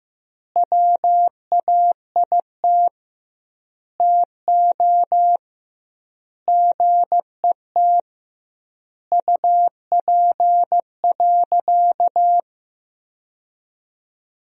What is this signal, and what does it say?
Morse code, heard as a keyed tone, about 16 wpm, 700 Hz. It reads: WAIT TO GET UP.
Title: 2x Word-Spacing - Sentences from Top 500 Words - 15wpm